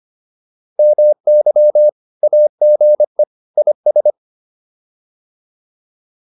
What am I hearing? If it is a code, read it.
Morse code, heard as a keyed tone, about 25 wpm, 600 Hz. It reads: MY AGE IS